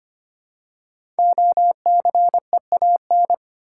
Morse code, heard as a keyed tone, about 26 wpm, 700 Hz. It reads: OCEAN
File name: 5 Letter Words - First 3 Letters Given - 25wpm